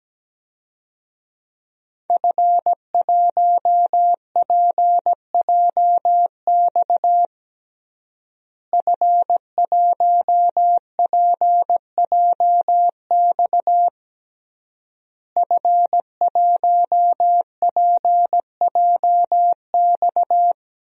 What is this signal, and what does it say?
Morse code, heard as a keyed tone, about 17 wpm, 700 Hz. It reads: F1PJX F1PJX F1PJX
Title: DXCC-100 Call Signs - 5 Characters - Repeated 3x - 17wpm